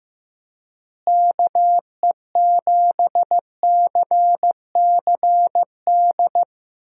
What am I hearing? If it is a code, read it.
Morse code, heard as a keyed tone, about 15 wpm, 700 Hz. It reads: KE7CCD
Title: USA Call Signs - 6 Characters - Repeated 2x - 15wpm